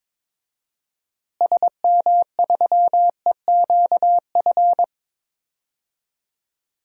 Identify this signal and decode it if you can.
Morse code, heard as a keyed tone, about 22 wpm, 700 Hz. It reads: SM3EQF
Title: DXCC-100 Call Signs - 6 Characters - Repeated 3x - 22wpm